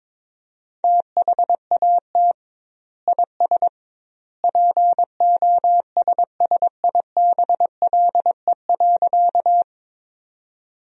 Morse code, heard as a keyed tone, about 21 wpm, 700 Hz. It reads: THAT IS POSSIBLE.
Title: 2x Word-Spacing - Sentences from Top 500 Words - 22wpm